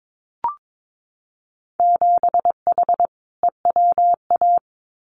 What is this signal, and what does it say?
Morse code, heard as a keyed tone, about 22 wpm, 700 Hz. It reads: 7H EWA
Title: Winter Field Day Exchanges Encore 22wpm